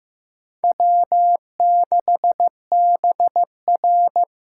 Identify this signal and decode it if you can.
Morse code, heard as a keyed tone, about 15 wpm, 700 Hz. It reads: W6BR
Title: USA Call Signs - 6x Character Spacing - 15wpm